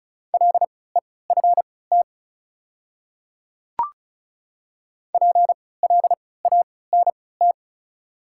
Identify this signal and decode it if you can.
Morse code, heard as a keyed tone, about 36 wpm, spaced as for 20 wpm, 700 Hz. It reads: LEFT PLANT
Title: 3x Character Spacing - Top 300 Words - 35wpm